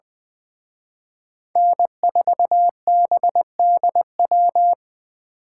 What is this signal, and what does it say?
Morse code, heard as a keyed tone, about 24 wpm, 700 Hz. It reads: N4BDW
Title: Speed Racing - USA Call Signs 20wpm